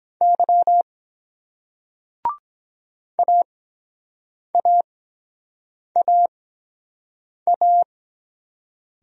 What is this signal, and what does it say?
Morse code, heard as a keyed tone, about 23 wpm, spaced as for 7 wpm, 700 Hz. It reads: Y AAAA